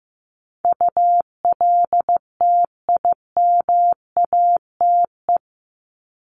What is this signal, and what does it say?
Morse code, heard as a keyed tone, about 15 wpm, 700 Hz. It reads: ULTIMATE
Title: LY Suffix - Top 100 Words with Base Words 15wpm